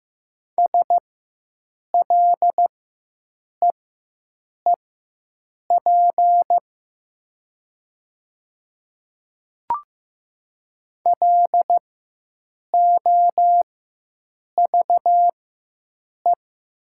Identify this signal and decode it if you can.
Morse code, heard as a keyed tone, about 15 wpm, spaced as for 7 wpm, 700 Hz. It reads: SLEEP LOVE
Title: 4x Character Spacing - Top 500 Words - 15wpm